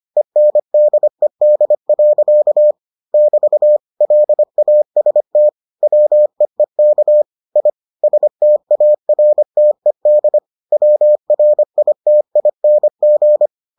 Morse code, heard as a keyed tone, about 25 wpm, 600 Hz. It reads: ENDED. <BT> LAST WEEK I STARTED WRITING